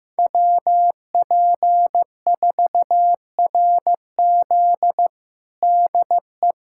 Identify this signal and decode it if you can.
Morse code, heard as a keyed tone, about 16 wpm, 700 Hz. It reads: WP4RZ DE
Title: USA Callsign DE Callsign - 5 Characters - 15wpm